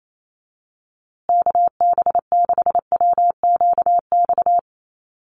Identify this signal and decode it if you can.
Morse code, heard as a keyed tone, about 28 wpm, 690 Hz. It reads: KB6WQX
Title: USA Call Signs Repeated 3x 28wpm